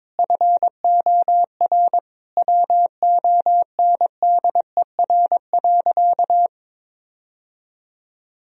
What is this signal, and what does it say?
Morse code, heard as a keyed tone, about 22 wpm, 700 Hz. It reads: FOR WONDER.
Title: Sentences from Top 1000 Words - Review Part 3 - 22wpm